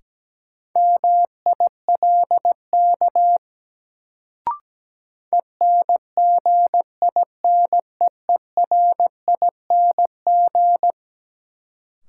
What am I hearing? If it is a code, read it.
Morse code, heard as a keyed tone, about 17 wpm, 700 Hz. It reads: MILK ENGINEERING
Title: Sentences from Top 1500 Words - New Word Drill Alt - 17wpm